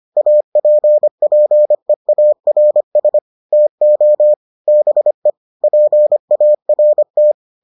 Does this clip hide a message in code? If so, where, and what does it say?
Yes: Morse code from 0.2 to 7.3 s, heard as a keyed tone, about 25 wpm, 600 Hz. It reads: APPEARS TO BE PART